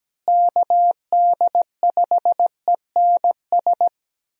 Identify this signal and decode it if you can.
Morse code, heard as a keyed tone, about 17 wpm, 700 Hz. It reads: KD5ENS
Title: USA Call Signs - 6 Characters - Repeated 2x - 17wpm